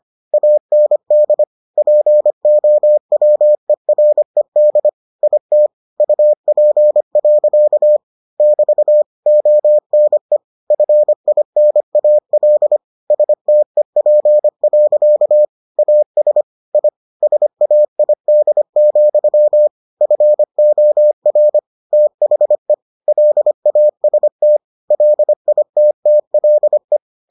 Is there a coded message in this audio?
Yes: Morse code from 0.3 to 27.0 s, heard as a keyed tone, about 25 wpm, 600 Hz. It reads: AND POWERED IT UP. <BT> ONE FINAL STEP. AS I SAID, FOR THE LAST LITTLE